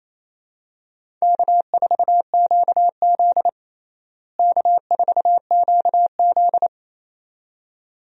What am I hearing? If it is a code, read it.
Morse code, heard as a keyed tone, about 28 wpm, 700 Hz. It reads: K4QZ K4QZ
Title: USA Call Signs - 4 Characters - Repeated 2x - 28wpm